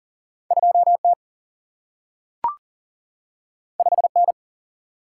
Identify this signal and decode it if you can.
Morse code, heard as a keyed tone, about 41 wpm, 700 Hz. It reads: 2T 5N
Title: Single-Letter Single-Number 40wpm